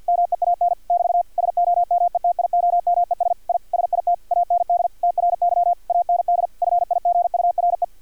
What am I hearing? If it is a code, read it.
Morse code, heard as a keyed tone, about 50 wpm, 700 Hz, with radio noise. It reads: MEAN <BT> SOMETIOMES I SIT AND TRY AND FIGURE
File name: archived / CRSnet